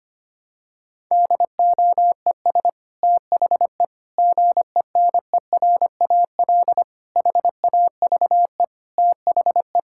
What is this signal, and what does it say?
Morse code, heard as a keyed tone, about 25 wpm, 700 Hz. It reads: DOES THE GENERAL HAVE THE
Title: Sentences from Top 1000 Words - Review Part 6 - 25wpm